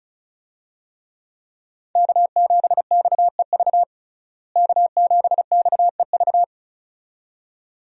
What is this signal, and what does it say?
Morse code, heard as a keyed tone, about 34 wpm, 700 Hz. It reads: K7XEV K7XEV
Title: USA Call Signs - 5 Characters - Repeated 2x - 35wpm